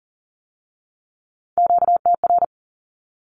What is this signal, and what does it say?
Morse code, heard as a keyed tone, about 41 wpm, 700 Hz. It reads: QTR
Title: Q-Codes with Spoken Meaning - 40wpm